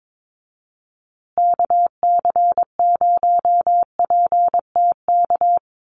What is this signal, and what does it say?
Morse code, heard as a keyed tone, about 22 wpm, 700 Hz. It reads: KC0PTK
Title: USA Call Signs Repeated 2x 22wpm